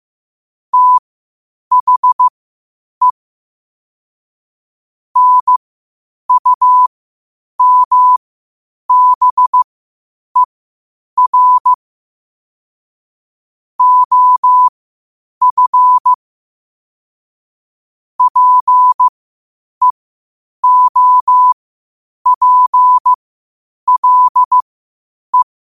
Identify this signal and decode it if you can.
Morse code, heard as a keyed tone, about 15 wpm, spaced as for 8 wpm, 1000 Hz. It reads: THE NUMBER OF PEOPLE
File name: qotdfiles / 8wpm